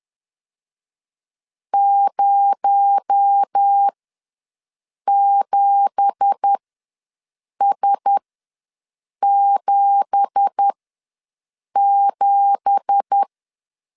Morse code, heard as a keyed tone, about 10 wpm, spaced as for 6 wpm, 800 Hz. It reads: 07S77
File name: LEZIONI / 9 2867 ws 10wpm 4215